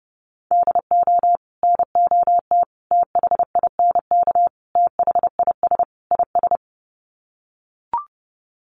Morse code, heard as a keyed tone, about 30 wpm, 700 Hz. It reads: DO NOT THINK THIS IS
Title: Sets of 5 Words Encore2 30wpm